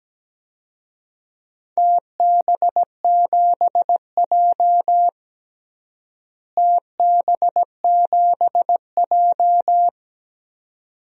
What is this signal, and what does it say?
Morse code, heard as a keyed tone, about 17 wpm, 700 Hz. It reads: TB7J TB7J
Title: DXCC-100 Call Signs - 4 Characters - Repeated 2x - 17wpm